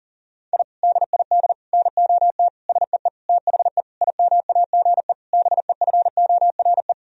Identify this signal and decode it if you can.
Morse code, heard as a keyed tone, about 40 wpm, 700 Hz. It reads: I DID NOT SEE THE IMAGE BEFORE
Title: Sentences from Top 1000 Words - Review Part 4 - 40wpm